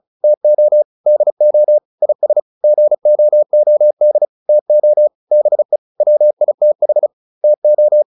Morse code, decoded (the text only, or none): TO DO IS GOOD TO BE WITH TO